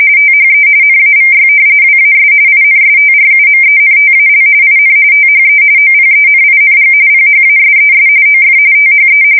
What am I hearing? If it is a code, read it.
RTTY, about 45.45 baud. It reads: THE NAVY WAS INSTRUMENTAL IN THE OPENING PHASES OF THE
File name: fsk170 / usnavy5